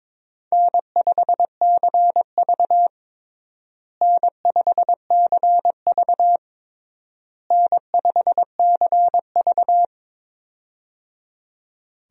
Morse code, heard as a keyed tone, about 22 wpm, 700 Hz. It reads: N5CV N5CV N5CV
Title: USA Call Signs - 4 Characters - Repeated 3x - 22wpm